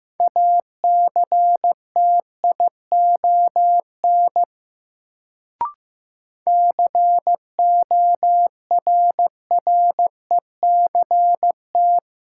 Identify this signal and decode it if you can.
Morse code, heard as a keyed tone, about 15 wpm, 700 Hz. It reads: ACTION CORRECT
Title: Sentences from Top 700 Words - New Word Drill - 15wpm